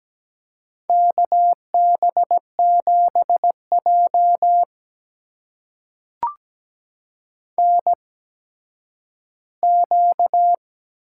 Morse code, heard as a keyed tone, about 17 wpm, spaced as for 10 wpm, 700 Hz. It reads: KB7J NQ
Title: USA Call Signs - 4 Characters - 8x Character Spacing - 17wpm